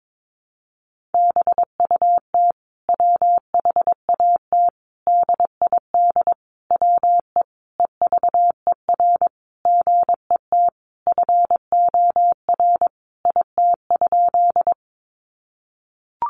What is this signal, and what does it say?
Morse code, heard as a keyed tone, about 22 wpm, 700 Hz. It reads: BUT WHAT DID WE EVER GET FOR IT?
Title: Sentences from Top 400 Words 22wpm